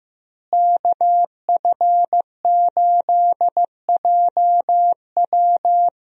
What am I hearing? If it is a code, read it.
Morse code, heard as a keyed tone, about 15 wpm, 700 Hz. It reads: KF8JW
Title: USA Call Signs - 5 Characters - Repeated 2x - 15wpm